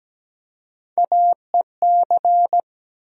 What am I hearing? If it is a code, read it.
Morse code, heard as a keyed tone, about 17 wpm, 700 Hz. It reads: AEC
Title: USA Call Sign Postfixes - 3 Letters 17wpm